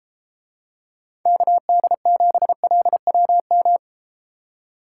Morse code, heard as a keyed tone, about 33 wpm, 700 Hz. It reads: KD7LWM